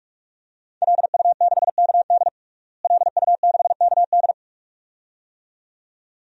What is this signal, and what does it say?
Morse code, heard as a keyed tone, about 46 wpm, 700 Hz. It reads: LU6KD LU6KD
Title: DXCC-100 Call Signs - 5 Characters - Repeated 2x - 45wpm